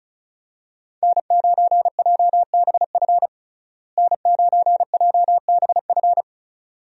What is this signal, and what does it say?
Morse code, heard as a keyed tone, about 36 wpm, 700 Hz. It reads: N9JBF N9JBF
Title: USA Call Signs - 5 Characters - Repeated 2x - 35wpm